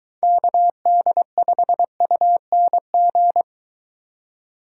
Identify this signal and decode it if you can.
Morse code, heard as a keyed tone, about 21 wpm, 700 Hz. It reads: KD5UNG